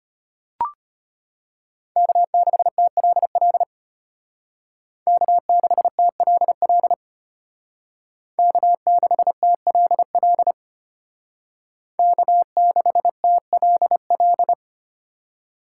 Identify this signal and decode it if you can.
Morse code, heard as a keyed tone, about 32 wpm, 700 Hz. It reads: K6TLL K6TLL K6TLL K6TLL